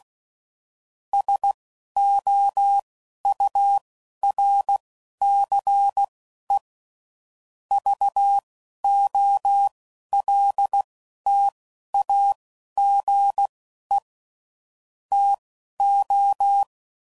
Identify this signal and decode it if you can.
Morse code, heard as a keyed tone, about 16 wpm, spaced as for 11 wpm, 800 Hz. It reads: SOURCE VOLTAGE TO
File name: CW Code Groups @10WPM / Technical Subjects / XC